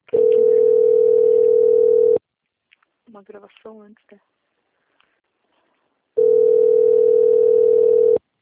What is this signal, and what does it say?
Ringback tone (North America), 440 + 480 Hz, with speech.
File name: out- - - - - .66675